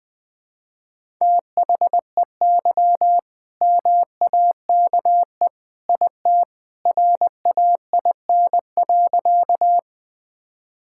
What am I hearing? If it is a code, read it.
Morse code, heard as a keyed tone, about 20 wpm, 700 Hz. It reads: THEY MAKE IT RAIN.
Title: All-Age Appropriate Jokes - 20wpm